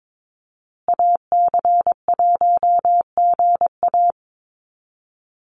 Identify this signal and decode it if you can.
Morse code, heard as a keyed tone, about 22 wpm, 700 Hz. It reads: AC1GA
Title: USA Call Signs Repeated 2x 22wpm